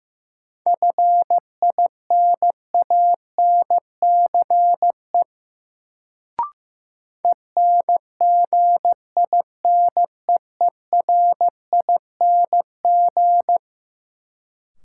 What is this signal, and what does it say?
Morse code, heard as a keyed tone, about 15 wpm, 700 Hz. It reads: FINANCE ENGINEERING
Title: Sentences from Top 1500 Words - New Word Drill - 15wpm